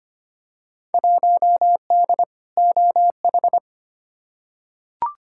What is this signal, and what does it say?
Morse code, heard as a keyed tone, about 25 wpm, 700 Hz. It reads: 1D OH
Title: Field Day Exchanges - 25wpm